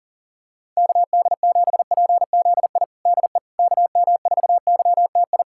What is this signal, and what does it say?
Morse code, heard as a keyed tone, about 40 wpm, 700 Hz. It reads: KD7PZI DE KM4YTI
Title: USA Callsign DE Callsign - 6 Characters - 40wpm